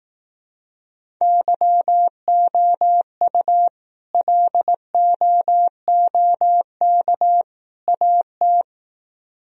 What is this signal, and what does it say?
Morse code, heard as a keyed tone, about 18 wpm, 700 Hz. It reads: YOU LOOK AT
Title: Speed-Racing - Sets of 3 Words 15wpm